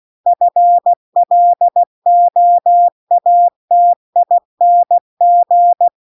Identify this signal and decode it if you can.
Morse code, heard as a keyed tone, about 16 wpm, 700 Hz. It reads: FLOATING